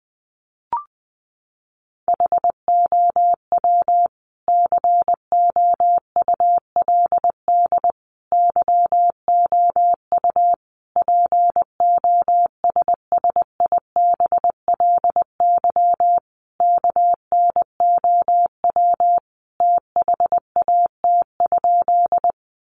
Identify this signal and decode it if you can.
Morse code, heard as a keyed tone, about 20 wpm, 700 Hz. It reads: HOW COULD YOU POSSIBLY KNOW THAT?
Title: Sentences from Top 1300 Words 20wpm